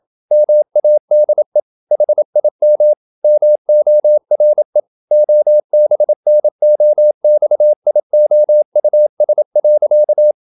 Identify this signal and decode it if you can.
Morse code, heard as a keyed tone, about 27 wpm, 600 Hz. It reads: MADE HIM MORE OBNOXIOUS.
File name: Never 0027wpm